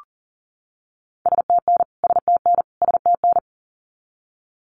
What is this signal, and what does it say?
Morse code, heard as a keyed tone, about 40 wpm, 700 Hz. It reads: STN STN STN
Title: Abbreviations Repeated 3x - 40wpm